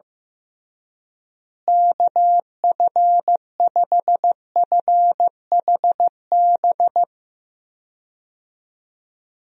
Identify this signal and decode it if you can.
Morse code, heard as a keyed tone, about 15 wpm, 700 Hz. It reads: KF5FHB
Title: USA Call Signs - 6 Characters - Repeated 3x - 15wpm